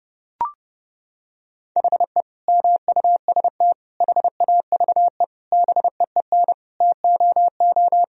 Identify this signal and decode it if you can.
Morse code, heard as a keyed tone, about 32 wpm, 700 Hz. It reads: HE MUST HAVE BEEN TOO
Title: Sentences from Top 1000 Words - Review Part 3 - 30wpm